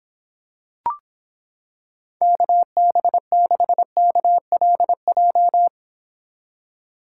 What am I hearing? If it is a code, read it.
Morse code, heard as a keyed tone, about 26 wpm, 700 Hz. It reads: KB6KLJ